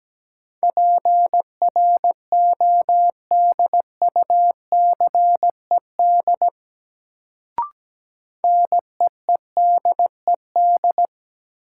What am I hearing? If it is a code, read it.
Morse code, heard as a keyed tone, about 17 wpm, 700 Hz. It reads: PRODUCED NEEDED
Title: Sentences from Top 1000 Words - Review Part 6 - New Word Drill - 17wpm